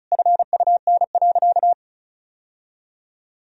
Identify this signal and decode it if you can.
Morse code, heard as a keyed tone, about 36 wpm, 700 Hz. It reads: FUN.
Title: Sentences from Top 1000 Words - Review Part 4 - 35wpm